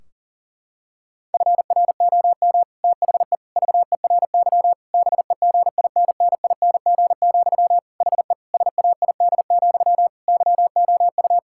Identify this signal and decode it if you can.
Morse code, heard as a keyed tone, about 40 wpm, 700 Hz. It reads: FROM THE VERY BEGINNING, HE SAID, YOU
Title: Sentences from Top 1600 Words 40wpm